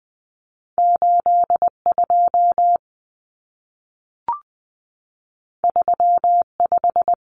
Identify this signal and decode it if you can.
Morse code, heard as a keyed tone, about 20 wpm, 700 Hz. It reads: 82 35
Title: Common Cut Numbers - 2-Digit 20wpm